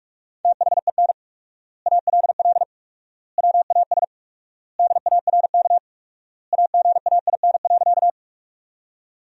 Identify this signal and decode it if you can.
Morse code, heard as a keyed tone, about 41 wpm, 690 Hz. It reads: THEN ALL WAS DARK AGAIN.
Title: 4x Word-Spacing - Sentences from Top 500 Words - 45wpm